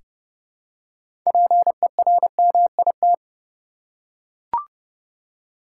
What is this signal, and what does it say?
Morse code, heard as a keyed tone, about 30 wpm, 700 Hz. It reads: PERMIT
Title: Six Letter Words - 30wpm